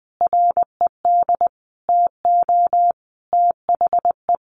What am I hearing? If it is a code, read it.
Morse code, heard as a keyed tone, about 20 wpm, 700 Hz. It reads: RED TO THE
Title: A Sound Of Thunder 20wpm